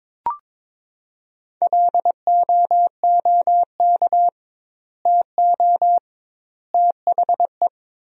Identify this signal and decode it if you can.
Morse code, heard as a keyed tone, about 22 wpm, 700 Hz. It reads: LOOK TO THE
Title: 2x Word-Spacing - Sentences from Top 400 Words - 22wpm